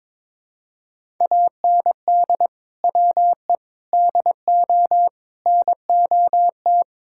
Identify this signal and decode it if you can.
Morse code, heard as a keyed tone, about 22 wpm, 700 Hz. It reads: AND WE DO NOT
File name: practice / 22wpm / Sentences with 4-Word Phrases - 22wpm